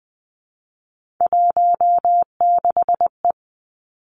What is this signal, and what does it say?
Morse code, heard as a keyed tone, about 20 wpm, 700 Hz. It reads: 16E
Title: Three Challenging Letter-Number Combinations 20wpm